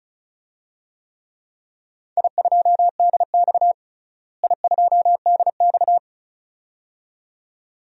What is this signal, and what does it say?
Morse code, heard as a keyed tone, about 36 wpm, 700 Hz. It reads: I2DX I2DX
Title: DXCC-100 Call Signs - 4 Characters - Repeated 2x - 35wpm